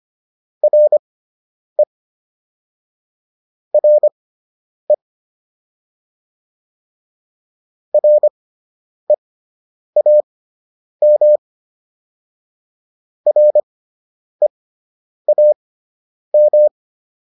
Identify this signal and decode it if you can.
Morse code, heard as a keyed tone, about 25 wpm, spaced as for 9 wpm, 600 Hz. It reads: RE RE REAM REAM